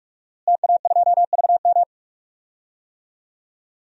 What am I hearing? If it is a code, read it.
Morse code, heard as a keyed tone, about 46 wpm, 700 Hz. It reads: TA2VM
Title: DXCC-100 Call Signs - 5 Characters - Repeated 3x - 45wpm